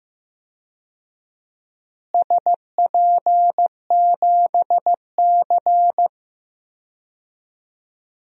Morse code, heard as a keyed tone, about 15 wpm, 700 Hz. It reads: SP7C
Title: DXCC-100 Call Signs - 4 Characters - Repeated 3x - 15wpm